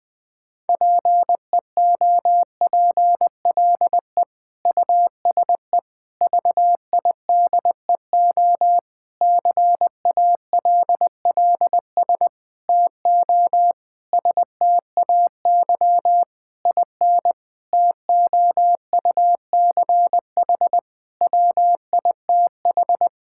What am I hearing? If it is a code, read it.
Morse code, heard as a keyed tone, about 20 wpm, 700 Hz. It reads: PEOPLE USE VIDEO CALLS TO STAY IN TOUCH WITH
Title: Difficult Sentences with Hints - 20wpm